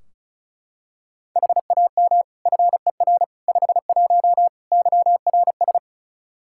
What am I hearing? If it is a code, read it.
Morse code, heard as a keyed tone, about 36 wpm, 700 Hz. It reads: HAM FER 51 YRS
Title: Practice QSO - Exchange 3 - More Info - 35wpm